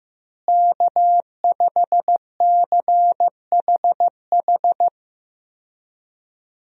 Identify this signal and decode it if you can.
Morse code, heard as a keyed tone, about 15 wpm, 700 Hz. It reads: K5CHH
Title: USA Call Signs - 5 Characters - Repeated 3x - 15wpm